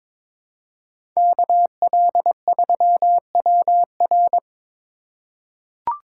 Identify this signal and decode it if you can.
Morse code, heard as a keyed tone, about 21 wpm, 700 Hz. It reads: KL3WR
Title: USA Call Signs - 5 Characters - 2x Character Spacing - 22wpm